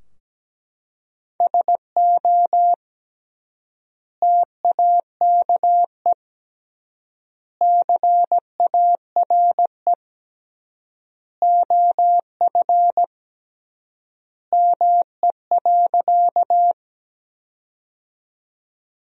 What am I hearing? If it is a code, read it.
Morse code, heard as a keyed tone, about 17 wpm, 700 Hz. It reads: SO TAKE CARE OF ME.
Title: 3x Word-Spacing - Sentences from Top 300 Words - 17wpm